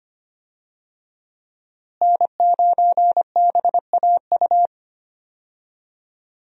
Morse code, heard as a keyed tone, about 25 wpm, 700 Hz. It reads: N9BAU
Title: USA Call Signs - 5 Characters - Repeated 3x - 25wpm